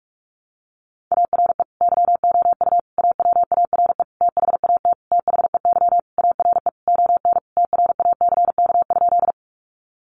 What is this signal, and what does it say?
Morse code, heard as a keyed tone, about 44 wpm, 700 Hz. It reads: ARE YOU AWARE THAT THEY ARE ON TRACK?
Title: Sentences from Top 1200 Words 45wpm